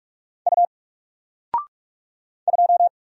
Morse code, heard as a keyed tone, about 46 wpm, 700 Hz. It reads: U 2
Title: Single Letter-Number Alt 45wpm